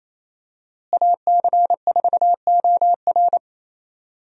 Morse code, heard as a keyed tone, about 28 wpm, 700 Hz. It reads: AC4OR
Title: USA Call Signs - 5 Characters - Repeated 3x - 28wpm